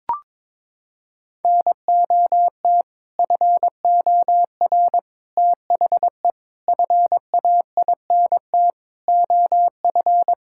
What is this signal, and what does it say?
Morse code, heard as a keyed tone, about 22 wpm, 700 Hz. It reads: NOT FOR THE FAINT OF
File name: practice / 22wpm / Sets of 6 Words - 22wpm